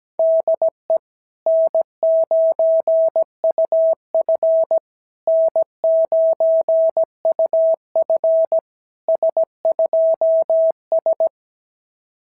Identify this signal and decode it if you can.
Morse code, heard as a keyed tone, about 17 wpm, 660 Hz. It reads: DE N9UF N9UF S2S
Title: SOTA QSO Practice 17wpm